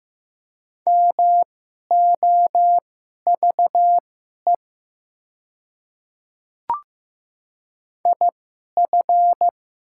Morse code, heard as a keyed tone, about 15 wpm, spaced as for 11 wpm, 690 Hz. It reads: MOVE IF
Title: 2x Character Spacing - Top 300 Words - 15wpm